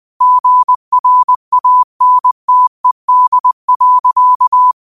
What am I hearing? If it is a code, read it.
Morse code, heard as a keyed tone, about 20 wpm, 1000 Hz. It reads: GRANTED.
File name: qotdfiles / 20wpm